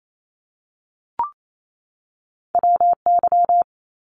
Morse code, heard as a keyed tone, about 28 wpm, 700 Hz. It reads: WY
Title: ARRL Sections 28wpm